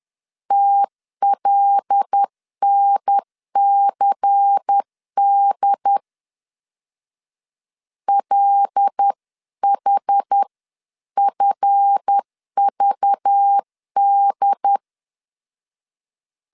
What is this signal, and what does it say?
Morse code, heard as a keyed tone, about 10 wpm, 800 Hz. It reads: TLNCD LHFVD
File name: LEZIONI / 7 GF9Z / 7 GF 10wpm 2514